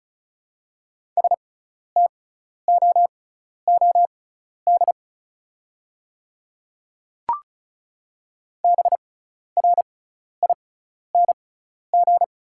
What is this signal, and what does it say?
Morse code, heard as a keyed tone, about 34 wpm, spaced as for 12 wpm, 700 Hz. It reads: STOOD BRING
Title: 6x Character Spacing - Top 500 Words - 35wpm